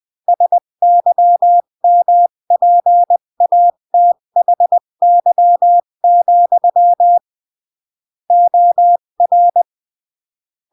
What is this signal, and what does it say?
Morse code, heard as a keyed tone, about 20 wpm, 700 Hz. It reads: SYMPATHY, OR
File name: CW / jodorowsky / 82 commandments 20-15